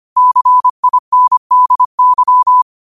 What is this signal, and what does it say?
Morse code, heard as a keyed tone, about 25 wpm, 1000 Hz. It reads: CINDY